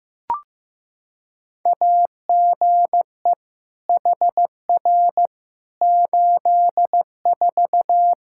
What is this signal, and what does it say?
Morse code, heard as a keyed tone, about 15 wpm, 700 Hz. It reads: AGE HR 84
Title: Practice QSO - Exchange 3 - More Info - 15wpm